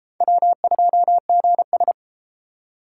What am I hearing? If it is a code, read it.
Morse code, heard as a keyed tone, about 29 wpm, 700 Hz. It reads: W2GS